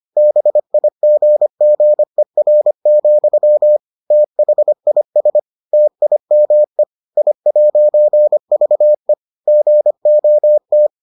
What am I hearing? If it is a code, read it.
Morse code, heard as a keyed tone, about 25 wpm, 600 Hz. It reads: BIGGER, THIS TIME I'VE GOT